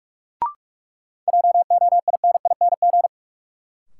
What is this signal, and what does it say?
Morse code, heard as a keyed tone, about 47 wpm, 700 Hz. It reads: JOINING
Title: Sets of 5 Words - New Word Drill - 45wpm